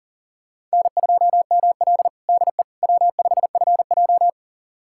Morse code, heard as a keyed tone, about 40 wpm, 700 Hz. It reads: N2ML DE W5FJ
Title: USA Callsign DE Callsign - 4 Characters - 40wpm